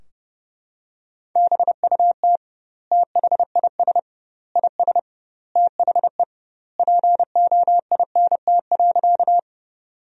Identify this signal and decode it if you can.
Morse code, heard as a keyed tone, about 32 wpm, 700 Hz. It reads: BUT THIS IS THE POINT.
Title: 2x Word-Spacing - Sentences from Top 200 Words - 30wpm